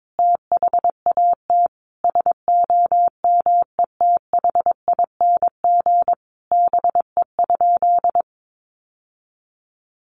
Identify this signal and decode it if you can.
Morse code, heard as a keyed tone, about 22 wpm, 700 Hz. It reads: THAT SOMETHING BE?
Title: Sentences from Top 600 Words 22wpm